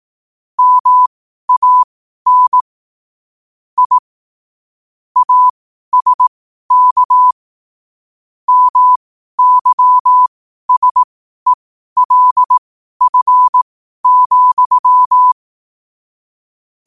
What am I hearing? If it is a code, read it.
Morse code, heard as a keyed tone, about 18 wpm, spaced as for 12 wpm, 1000 Hz. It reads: MAN I ASK MYSELF,